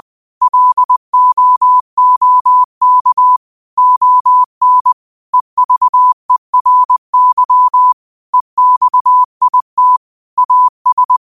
Morse code, heard as a keyed tone, about 20 wpm, 1000 Hz. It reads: LOOK ON EVERY EXIT AS
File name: 20wpm